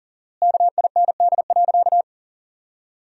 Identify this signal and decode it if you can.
Morse code, heard as a keyed tone, about 40 wpm, 700 Hz. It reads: KIND.
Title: 4x Word-Spacing - Sentences from Top 500 Words - 40wpm